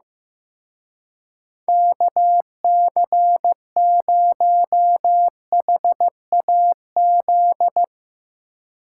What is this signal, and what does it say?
Morse code, heard as a keyed tone, about 16 wpm, 700 Hz. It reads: KC0HAZ
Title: Speed Racing - USA 2x3 Call Signs 15wpm